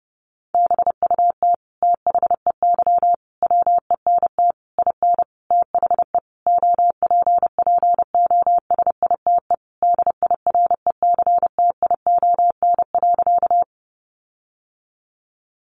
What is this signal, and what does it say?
Morse code, heard as a keyed tone, about 30 wpm, 700 Hz. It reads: BUT THEY WENT IN THE OPPOSITE DIRECTION.
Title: Sentences from Top 1300 Words 30wpm